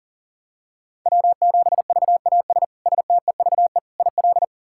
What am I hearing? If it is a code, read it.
Morse code, heard as a keyed tone, about 40 wpm, 700 Hz. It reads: W7VAS STEVE IL
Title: K1USN SST QSO Practice - 40wpm